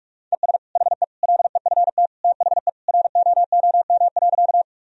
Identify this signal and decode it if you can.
Morse code, heard as a keyed tone, about 42 wpm, 700 Hz. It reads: ES HE LEFT THE ROOM.
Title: Sentences from Top 300 Words and Abbreviations 45wpm